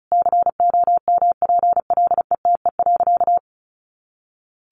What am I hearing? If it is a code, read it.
Morse code, heard as a keyed tone, about 34 wpm, 700 Hz. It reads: COMPLETE.
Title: Sentences from Top 400 Words 35wpm